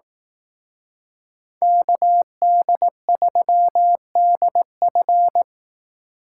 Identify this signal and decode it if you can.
Morse code, heard as a keyed tone, about 18 wpm, 700 Hz. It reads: KD3DF
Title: Speed Racing - USA 2x2 Call Signs 15wpm